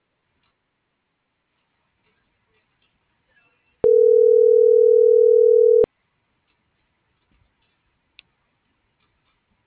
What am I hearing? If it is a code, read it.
Ringback tone (North America), 440 + 480 Hz.